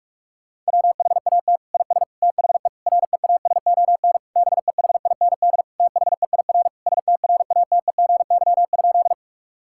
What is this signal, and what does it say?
Morse code, heard as a keyed tone, about 46 wpm, 690 Hz. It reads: WHAT IS THE REASON BEHIND THEIR STRATEGY?
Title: Sentences from Top 1000 Words - Review Part 5 - 45wpm